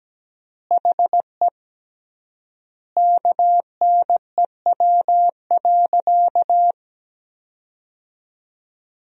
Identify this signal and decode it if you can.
Morse code, heard as a keyed tone, about 17 wpm, 700 Hz. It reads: HE KNEW.
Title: 3x Word-Spacing - Sentences from Top 400 Words - 17wpm